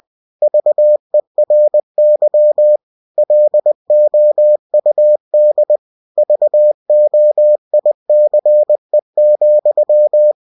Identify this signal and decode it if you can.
Morse code, heard as a keyed tone, about 20 wpm, 600 Hz. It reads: VERY LOUD VOICE,
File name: cw-books / winnie-the-pooh / winnie-20wpm-page-0029